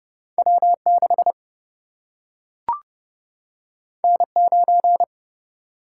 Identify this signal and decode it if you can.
Morse code, heard as a keyed tone, about 30 wpm, 700 Hz. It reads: W6 N9
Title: Single-Letter Single-Number 30wpm